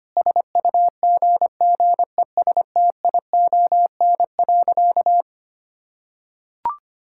Morse code, heard as a keyed tone, about 25 wpm, 700 Hz. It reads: SUGGESTION.
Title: Sentences from Top 1600 Words 25wpm